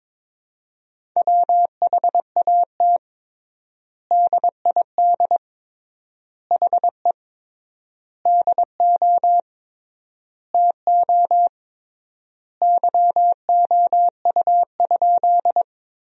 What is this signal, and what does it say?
Morse code, heard as a keyed tone, about 21 wpm, 700 Hz. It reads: WHAT DID HE DO TO YOU?
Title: 3x Word-Spacing - Sentences from Top 100 Words - 22wpm